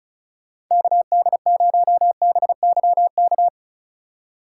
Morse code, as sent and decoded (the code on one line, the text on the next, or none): -.- -.. ----- -... -.-- -.-
KD0BYK